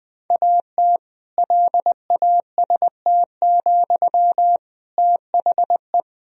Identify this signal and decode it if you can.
Morse code, heard as a keyed tone, about 20 wpm, 700 Hz. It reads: AT LAST, THE
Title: Sentences from Top 1000 Words - Review Part 2 - 20wpm